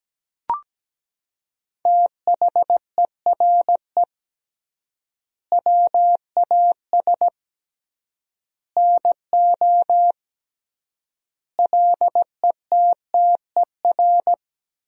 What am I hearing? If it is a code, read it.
Morse code, heard as a keyed tone, about 17 wpm, 700 Hz. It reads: THERE WAS NO LETTER
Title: 3x Word-Spacing - Sentences from Top 300 Words - 17wpm